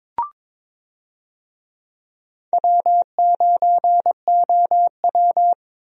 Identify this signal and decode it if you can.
Morse code, heard as a keyed tone, about 22 wpm, 700 Hz. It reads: W9OW
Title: USA Call Signs - 4 Characters - Repeated 2x - 22wpm